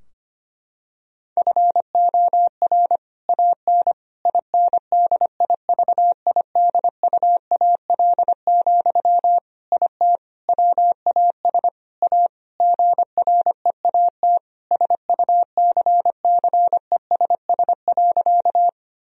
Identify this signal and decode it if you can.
Morse code, heard as a keyed tone, about 25 wpm, 700 Hz. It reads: FOR AN INDIVIDUAL, IT WAS A GREAT SUCCESS.
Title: Sentences from Top 1000 Words - Review Part 4 - 25wpm